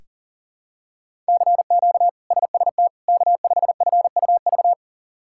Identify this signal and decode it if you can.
Morse code, heard as a keyed tone, about 40 wpm, 700 Hz. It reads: CQ SST K5FUV
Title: K1USN SST QSO Practice - 40wpm